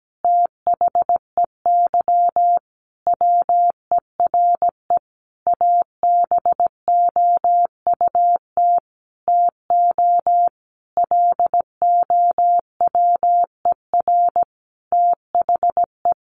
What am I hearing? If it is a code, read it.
Morse code, heard as a keyed tone, about 17 wpm, 700 Hz. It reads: THEY WERE ABOUT TO LOWER THE
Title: Sentences from Top 700 Words 17wpm